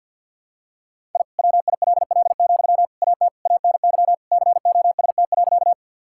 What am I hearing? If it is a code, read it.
Morse code, heard as a keyed tone, about 51 wpm, 700 Hz. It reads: I WILL, AT ANY COST.
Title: Sentences from Top 1000 Words - Review Part 5 - 50wpm